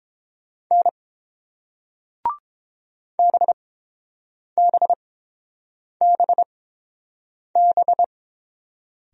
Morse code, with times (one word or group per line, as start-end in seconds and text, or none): 0.7-0.9 N
3.2-8.0 BBBB